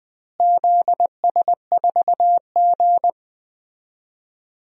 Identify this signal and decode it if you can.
Morse code, heard as a keyed tone, about 20 wpm, 700 Hz. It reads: ZS4G
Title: DXCC-100 Call Signs - 4 Characters - Repeated 2x - 20wpm